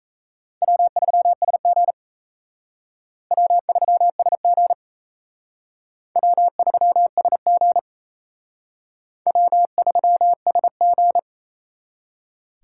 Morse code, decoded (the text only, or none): W3SG W3SG W3SG W3SG